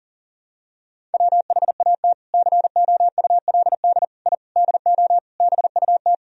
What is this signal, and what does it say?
Morse code, heard as a keyed tone, about 40 wpm, 700 Hz. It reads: WHAT COULD I DO BUT
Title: Sentences from Top 1000 Words - Review Part 1 - 40wpm